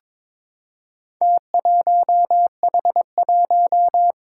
Morse code, heard as a keyed tone, about 22 wpm, 700 Hz. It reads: T1H1
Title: Four Challenging Letter-Number Combinations 22wpm